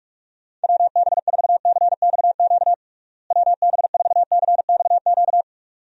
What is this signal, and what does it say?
Morse code, heard as a keyed tone, about 43 wpm, 700 Hz. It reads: WB4CXQ WB4CXQ
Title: USA Call Signs - 6 Characters - Repeated 2x - 45wpm